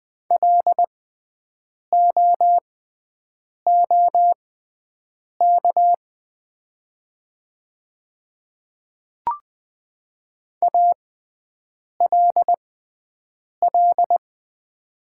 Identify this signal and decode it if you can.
Morse code, heard as a keyed tone, about 20 wpm, spaced as for 7 wpm, 700 Hz. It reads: LOOK ALL
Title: 6x Character Spacing - Top 100 Words - 20wpm